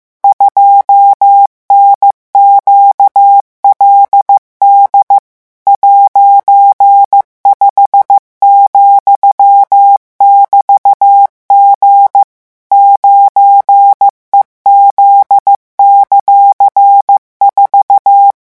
Morse code, heard as a keyed tone, about 15 wpm, 800 Hz. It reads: 2NQLD '5,<BT>G 9EZ;4